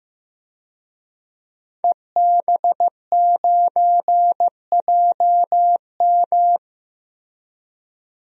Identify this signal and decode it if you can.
Morse code, heard as a keyed tone, about 15 wpm, 700 Hz. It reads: EB9JM
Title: DXCC-100 Call Signs - 5 Characters - 15wpm